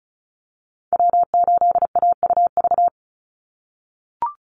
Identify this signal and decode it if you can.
Morse code, heard as a keyed tone, about 34 wpm, 700 Hz. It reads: W8AUV
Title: USA Call Signs 35wpm